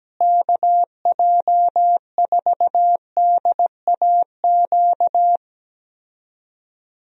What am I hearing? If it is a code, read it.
Morse code, heard as a keyed tone, about 17 wpm, 700 Hz. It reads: KJ4DAQ
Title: USA Call Signs - 6 Characters - Repeated 3x - 17wpm